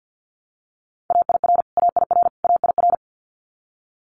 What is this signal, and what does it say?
Morse code, heard as a keyed tone, about 50 wpm, 690 Hz. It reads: AIR AIR AIR
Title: Multi-letter Phonemes Repeated 3x 50wpm